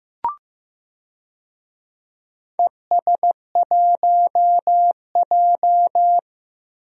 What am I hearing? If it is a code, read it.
Morse code, heard as a keyed tone, about 15 wpm, 700 Hz. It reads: ES1J
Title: DXCC-100 Call Signs - 4 Characters - Repeated 3x - 15wpm